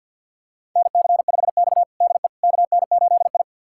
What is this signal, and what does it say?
Morse code, heard as a keyed tone, about 53 wpm, 700 Hz. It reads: NC5X DE KN8I
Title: USA Callsign DE Callsign - 4 Characters - 50wpm